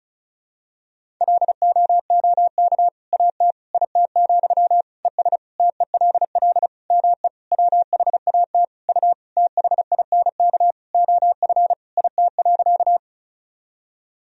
Morse code, heard as a keyed tone, about 36 wpm, 700 Hz. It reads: LOOK AT IT, ES TELL ME WHAT U THINK OF IT.
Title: Sentences from Top 200 Words and Abbreviations 35wpm